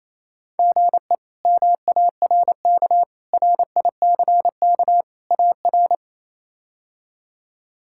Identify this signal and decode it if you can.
Morse code, heard as a keyed tone, about 28 wpm, 700 Hz. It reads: GE MARK RICK AR
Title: K1USN SST QSO Practice - 28wpm